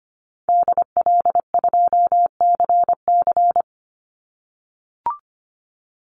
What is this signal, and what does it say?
Morse code, heard as a keyed tone, about 25 wpm, 700 Hz. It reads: DL2CC
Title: Call Signs - 25wpm